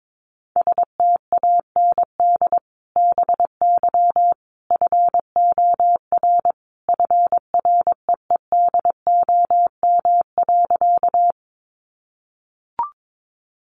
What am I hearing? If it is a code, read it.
Morse code, heard as a keyed tone, about 22 wpm, 700 Hz. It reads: STAND BY FOR FREEDOM.
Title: Sentences from Top 800 Words 22wpm